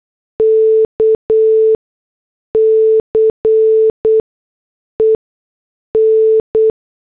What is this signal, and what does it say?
Morse code, heard as a keyed tone, about 8 wpm, 440 Hz. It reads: K C E N
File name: morse / headlines